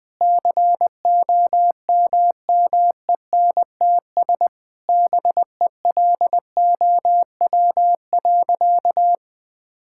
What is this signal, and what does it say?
Morse code, heard as a keyed tone, about 21 wpm, 700 Hz. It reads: COMMENTS BELOW.
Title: Sentences with 7-Word Phrases - Spoken Beginning - 20wpm